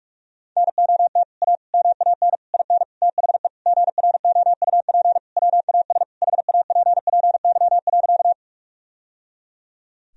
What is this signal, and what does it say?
Morse code, heard as a keyed tone, about 46 wpm, 700 Hz. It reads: NOT A MAN IN THE GROUP WAS HAPPY.
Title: Sentences from Top 1000 Words - Review Part 1 - 45wpm